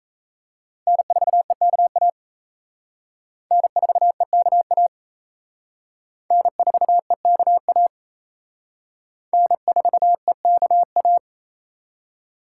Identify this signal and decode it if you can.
Morse code, heard as a keyed tone, about 35 wpm, 700 Hz. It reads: N4EKA N4EKA N4EKA N4EKA